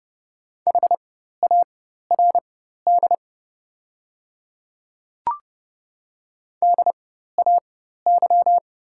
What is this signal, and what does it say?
Morse code, heard as a keyed tone, about 30 wpm, spaced as for 14 wpm, 700 Hz. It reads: HARD DAY
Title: 4x Character Spacing - Top 300 Words - 30wpm